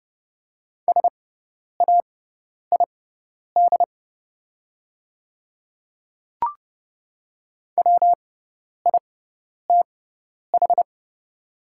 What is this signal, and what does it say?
Morse code, heard as a keyed tone, about 29 wpm, spaced as for 10 wpm, 700 Hz. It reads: SAID WITH
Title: 6x Character Spacing - Top 100 Words - 30wpm